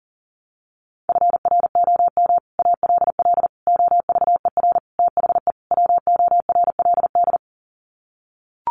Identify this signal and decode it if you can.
Morse code, heard as a keyed tone, about 40 wpm, 700 Hz. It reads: FROM ALL OVER THE WORLD
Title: Sets of 5 Words 40wpm